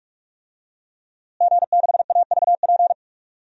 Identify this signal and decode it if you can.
Morse code, heard as a keyed tone, about 45 wpm, 700 Hz. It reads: G6AVP
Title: DXCC-100 Call Signs - 5 Characters - Repeated 3x - 45wpm